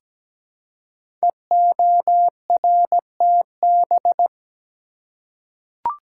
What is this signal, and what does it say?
Morse code, heard as a keyed tone, about 17 wpm, 700 Hz. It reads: EORTB
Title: Code Groups - 5 Letter:Numbers - 17wpm